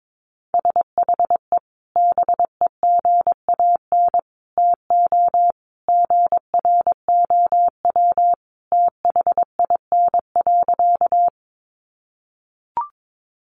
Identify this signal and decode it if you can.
Morse code, heard as a keyed tone, about 22 wpm, 700 Hz. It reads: SHE BEGAN TO GROW THIN.
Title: Sentences from Top 1200 Words 22wpm